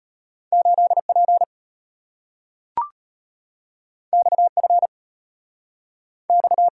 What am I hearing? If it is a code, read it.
Morse code, heard as a keyed tone, about 34 wpm, 700 Hz. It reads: 8P XF X